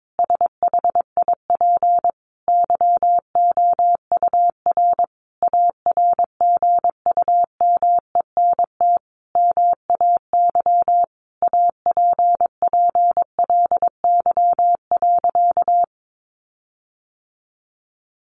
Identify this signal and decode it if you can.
Morse code, heard as a keyed tone, about 22 wpm, 700 Hz. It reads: SHIP YOUR ARGUMENT MAY APPLY.
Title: Sentences from Top 1400 Words 22wpm